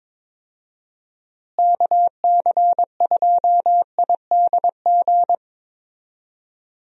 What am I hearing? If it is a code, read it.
Morse code, heard as a keyed tone, about 22 wpm, 700 Hz. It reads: KC2IDG
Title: USA Call Signs - 6 Characters - Repeated 2x - 22wpm